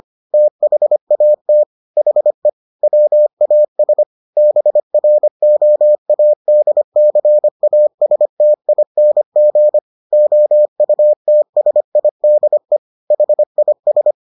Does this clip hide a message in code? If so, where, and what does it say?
Yes: Morse code from 0.3 to 14.1 s, heard as a keyed tone, about 25 wpm, 600 Hz. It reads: THAT HE WAS BROADCASTING OUTSIDE HIS